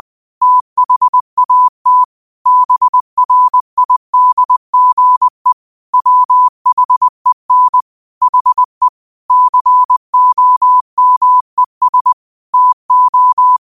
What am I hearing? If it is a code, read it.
Morse code, heard as a keyed tone, about 20 wpm, 1000 Hz. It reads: THAT BRIDGE WHEN HE COMES TO
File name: qotdfiles / 20wpm